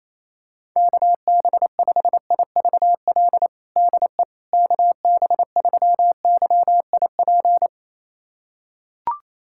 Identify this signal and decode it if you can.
Morse code, heard as a keyed tone, about 28 wpm, 700 Hz. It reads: KB5IVL DE KB3YIP
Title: USA Callsign DE Callsign - 6 Characters - 28wpm